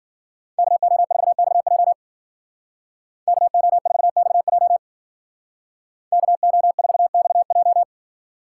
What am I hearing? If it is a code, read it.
Morse code, heard as a keyed tone, about 49 wpm, 700 Hz. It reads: KO4XJ KO4XJ KO4XJ
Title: Speed Racing - USA 2x2 Call Signs 40wpm